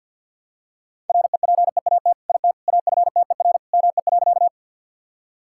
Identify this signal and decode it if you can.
Morse code, heard as a keyed tone, about 50 wpm, 700 Hz. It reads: REPEAT IT AFTER ME.
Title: Sentences from Top 1600 Words 50wpm